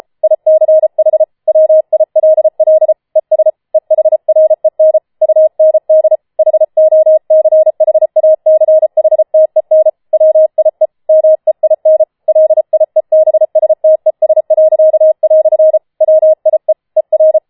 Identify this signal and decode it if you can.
Morse code, heard as a keyed tone, about 33 wpm, 630 Hz. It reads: ICH WILL ES EHREN UND HOCHACHTEN WIE MEIN LIEBSTES." WIE ER
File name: Schneewittchen Teil5 34wpm